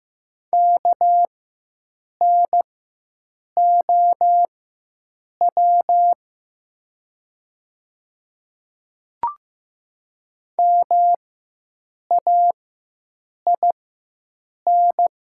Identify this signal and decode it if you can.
Morse code, heard as a keyed tone, about 15 wpm, spaced as for 7 wpm, 700 Hz. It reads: KNOW MAIN
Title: 4x Character Spacing - Top 300 Words - 15wpm